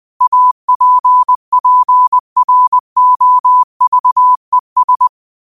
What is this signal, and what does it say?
Morse code, heard as a keyed tone, about 20 wpm, 1000 Hz. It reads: APPROVES